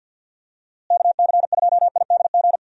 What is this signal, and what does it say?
Morse code, heard as a keyed tone, about 51 wpm, 700 Hz. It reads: KC2IDG
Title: USA Call Signs - 6 Characters - Repeated 2x - 50wpm